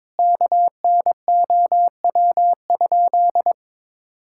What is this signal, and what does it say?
Morse code, heard as a keyed tone, about 21 wpm, 700 Hz. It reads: KNOW?
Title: Sentences from Top 1600 Words 22wpm